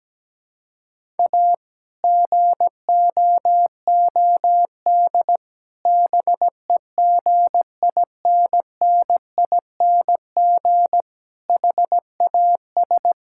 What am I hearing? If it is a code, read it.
Morse code, heard as a keyed tone, about 17 wpm, 700 Hz. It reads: A GOOD BEGINNING HAS
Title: Sentences from Top 1000 Words - Review Part 5 - 17wpm